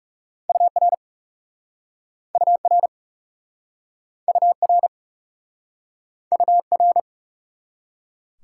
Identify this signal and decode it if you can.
Morse code, heard as a keyed tone, about 40 wpm, 700 Hz. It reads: UR UR UR UR